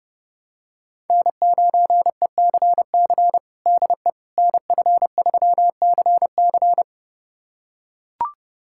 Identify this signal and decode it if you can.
Morse code, heard as a keyed tone, about 30 wpm, 700 Hz. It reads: N9ECC DE NF3CC
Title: USA Callsign DE Callsign 30wpm